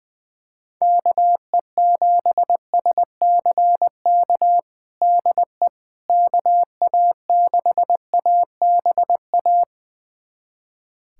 Morse code, heard as a keyed tone, about 21 wpm, 700 Hz. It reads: KE7SCK DE KA6ABA
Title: USA Callsign DE Callsign - 6 Characters - 20wpm